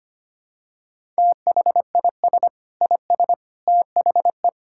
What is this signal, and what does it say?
Morse code, heard as a keyed tone, about 25 wpm, 700 Hz. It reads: THIS IS THE
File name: practice / 25wpm / Sentences with 4-Word Phrases - 25wpm